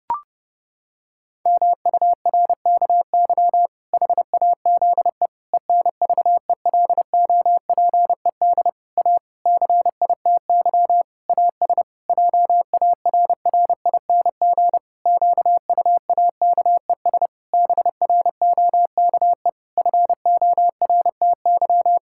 Morse code, heard as a keyed tone, about 30 wpm, 700 Hz. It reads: MURKY HAZE ENVELOPED A CITY AS JARRING QUAKES BROKE FORTY
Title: Pangram Challenge v2 - 30wpm